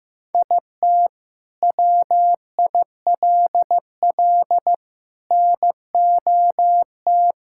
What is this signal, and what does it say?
Morse code, heard as a keyed tone, about 15 wpm, 700 Hz. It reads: IT WILL NOT
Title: Sentences from Top 1000 Words - Review Part 6 - 15wpm